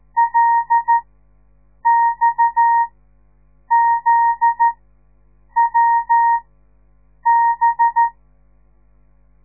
Morse code, heard as a keyed tone, about 13 wpm, spaced as for 7 wpm, 930 Hz. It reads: LXZWB
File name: radiomorningstar / morse / les 9